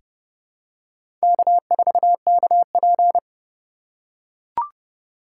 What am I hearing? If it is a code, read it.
Morse code, heard as a keyed tone, about 32 wpm, 700 Hz. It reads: K4KP
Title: USA Call Signs - 4 Characters - 3x Character Spacing - 30wpm